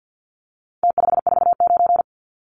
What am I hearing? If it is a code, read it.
Morse code, heard as a keyed tone, about 50 wpm, 700 Hz. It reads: T549
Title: Numbers - Contest Style - 4-Digit 50wpm